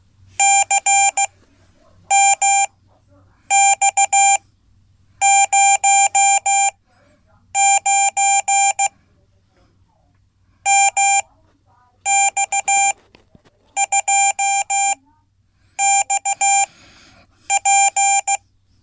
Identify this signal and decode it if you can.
Morse code, heard as a keyed tone, about 16 wpm, spaced as for 8 wpm, 790 Hz. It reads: CMX09 MX2XP
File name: Code / 5min-b